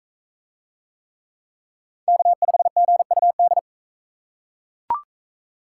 Morse code, heard as a keyed tone, about 42 wpm, 700 Hz. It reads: K5GUD